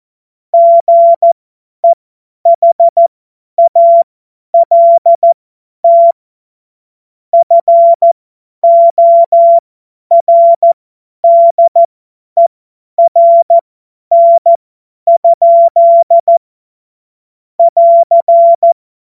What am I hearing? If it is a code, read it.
Morse code, heard as a keyed tone, about 14 wpm, spaced as for 10 wpm, 680 Hz. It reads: GEHALT FORDERN? <AR>
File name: Morsekurs / Witze / Witz 13. Dezember 2022 50 F70